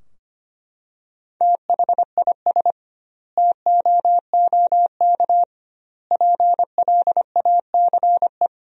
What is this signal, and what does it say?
Morse code, heard as a keyed tone, about 25 wpm, 700 Hz. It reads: THIS TOOK PLACE
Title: 2x Word-Spacing - Sentences from Top 300 Words - 25wpm